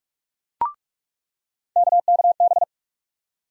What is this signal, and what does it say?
Morse code, heard as a keyed tone, about 42 wpm, 700 Hz. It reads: KKB
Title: USA Call Sign Postfixes - 3 Letters 45wpm